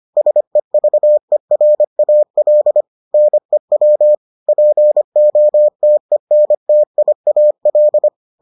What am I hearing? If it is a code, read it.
Morse code, heard as a keyed tone, about 25 wpm, 600 Hz. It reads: SEVERAL NEW POTENTIAL